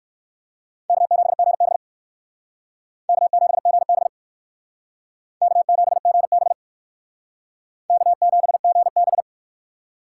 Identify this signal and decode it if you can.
Morse code, heard as a keyed tone, about 67 wpm, 700 Hz. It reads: K7GB K7GB K7GB K7GB